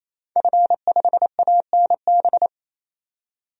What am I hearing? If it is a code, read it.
Morse code, heard as a keyed tone, about 28 wpm, 700 Hz. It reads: F5ANB
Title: DXCC-100 Call Signs - 5 Characters - 28wpm